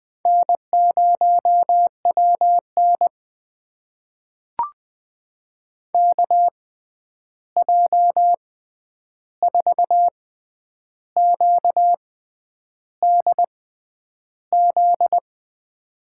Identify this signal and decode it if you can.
Morse code, heard as a keyed tone, about 20 wpm, spaced as for 9 wpm, 700 Hz. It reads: N0WN KJ4QDZ
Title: USA Call Signs - 6x Character Spacing - 20wpm